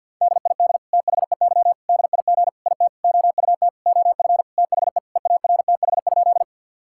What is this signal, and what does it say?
Morse code, heard as a keyed tone, about 50 wpm, 700 Hz. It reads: DID THEY DIG IT OUT OF THE EARTH?
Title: Sentences from Top 1600 Words 50wpm